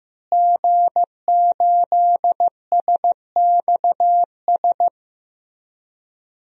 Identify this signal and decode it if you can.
Morse code, heard as a keyed tone, about 15 wpm, 700 Hz. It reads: G8SXS
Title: DXCC-100 Call Signs - 5 Characters - Repeated 3x - 15wpm